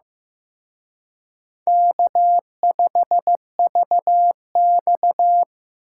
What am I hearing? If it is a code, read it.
Morse code, heard as a keyed tone, about 18 wpm, 700 Hz. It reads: K5VX
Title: Speed Racing - USA 1x2 Call Signs 15wpm